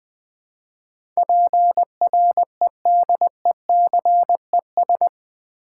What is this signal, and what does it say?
Morse code, heard as a keyed tone, about 20 wpm, 700 Hz. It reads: PREDECES
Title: Build A Word - 4 Letters At A Time - 11-Letter Words - 20wpm